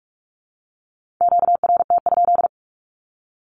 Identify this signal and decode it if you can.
Morse code, heard as a keyed tone, about 46 wpm, 700 Hz. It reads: QRT?
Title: Q-Codes with Spoken Meaning - 45wpm